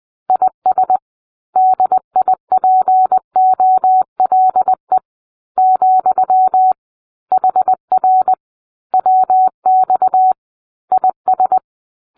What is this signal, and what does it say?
Morse code, heard as a keyed tone, about 20 wpm, 750 Hz. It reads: IS DIPOLE , HR WX IS